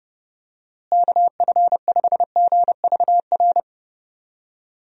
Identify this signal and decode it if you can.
Morse code, heard as a keyed tone, about 29 wpm, 700 Hz. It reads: KF5GVR
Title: USA Call Signs - 6 Characters - Repeated 3x - 30wpm